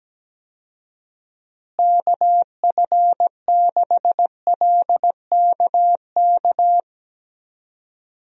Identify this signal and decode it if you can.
Morse code, heard as a keyed tone, about 17 wpm, 700 Hz. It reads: KF6LKK
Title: USA Call Signs - 6 Characters - Repeated 2x - 17wpm